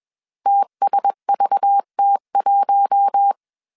Morse code, heard as a keyed tone, about 21 wpm, 800 Hz. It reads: TSVT1
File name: CW / DET / 1 6 1500C S B 20wpm 2048